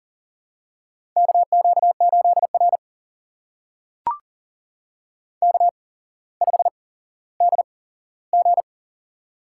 Morse code, heard as a keyed tone, about 39 wpm, spaced as for 17 wpm, 700 Hz. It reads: KQ8R K5DG
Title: USA Call Signs - 4 Characters - 8x Character Spacing - 40wpm